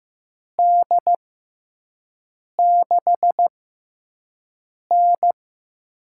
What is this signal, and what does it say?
Morse code, heard as a keyed tone, about 15 wpm, spaced as for 5 wpm, 700 Hz. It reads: D6N
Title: USA Call Signs - 4 Characters - 6x Character Spacing - 15wpm